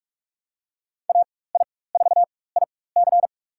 Morse code, heard as a keyed tone, about 42 wpm, spaced as for 20 wpm, 690 Hz. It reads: AI4IC
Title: USA Call Signs - 5 Characters - 4x Character Spacing - 45wpm